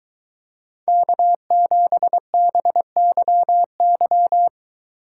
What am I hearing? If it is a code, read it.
Morse code, heard as a keyed tone, about 21 wpm, 700 Hz. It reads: K7BYY